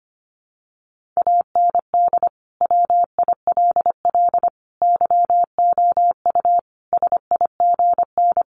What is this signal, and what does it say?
Morse code, heard as a keyed tone, about 25 wpm, 700 Hz. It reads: AND WILL YOU SIGN
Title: Sentences from Top 1400 Words 25wpm